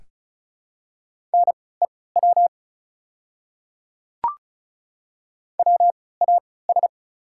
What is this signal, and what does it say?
Morse code, heard as a keyed tone, about 36 wpm, spaced as for 20 wpm, 690 Hz. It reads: NEW WAS
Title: 3x Character Spacing - Top 200 Words - 35wpm